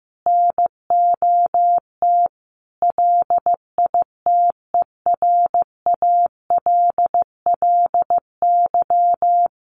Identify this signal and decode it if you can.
Morse code, heard as a keyed tone, about 15 wpm, 700 Hz. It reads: NOT LITERALLY
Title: Sentences from Top 1400 Words 15wpm